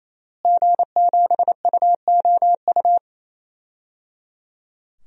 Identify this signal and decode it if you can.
Morse code, heard as a keyed tone, about 28 wpm, 700 Hz. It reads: G7UOU
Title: DXCC-100 Call Signs - 5 Characters - Repeated 3x - 28wpm